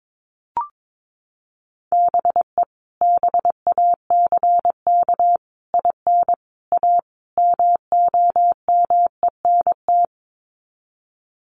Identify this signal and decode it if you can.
Morse code, heard as a keyed tone, about 22 wpm, 700 Hz. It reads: BE BACK IN A MOMENT
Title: Sets of 5 Words 22wpm